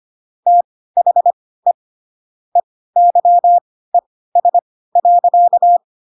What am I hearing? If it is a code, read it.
Morse code, heard as a keyed tone, about 25 wpm, spaced as for 16 wpm, 700 Hz. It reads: THE EYES.
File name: Pinocchio / Pinocchio ch3 10 25 ik1hge